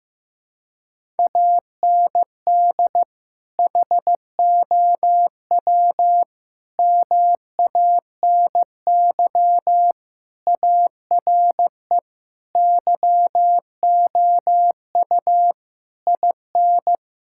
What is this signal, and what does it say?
Morse code, heard as a keyed tone, about 15 wpm, 700 Hz. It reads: AND HOW MANY ARE YOU IN
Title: Sentences from Top 1000 Words - Review Part 6 - 15wpm